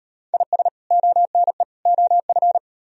Morse code, heard as a keyed tone, about 38 wpm, 700 Hz. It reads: IS ONE OF